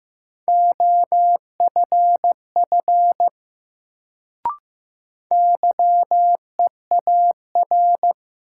Morse code, heard as a keyed tone, about 15 wpm, 700 Hz. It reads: OFF YEAR
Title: Sentences from Top 200 Words - New Word Drill - 15wpm